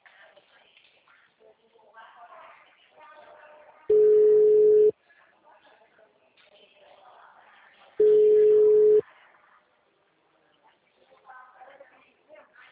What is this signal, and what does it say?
Ringback tone (Europe, 425 Hz), with speech.